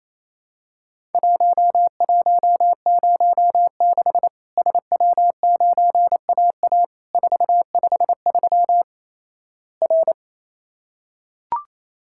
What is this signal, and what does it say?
Morse code, heard as a keyed tone, about 28 wpm, 700 Hz. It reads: 1106 SW9AA 453 R
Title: Worked All Europe DX Contest - QTC Exchange - 28wpm